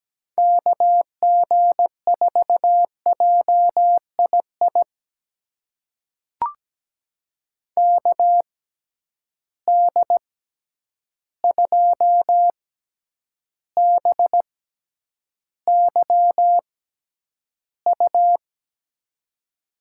Morse code, heard as a keyed tone, about 17 wpm, spaced as for 9 wpm, 700 Hz. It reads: KG4JII KD2BYU
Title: USA Call Signs - 6 Characters - 6x Character Spacing - 17wpm